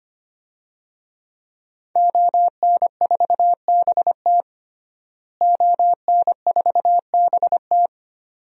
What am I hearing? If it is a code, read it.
Morse code, heard as a keyed tone, about 25 wpm, 700 Hz. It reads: ON4BT ON4BT
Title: DXCC-100 Call Signs - 5 Characters - Repeated 2x - 25wpm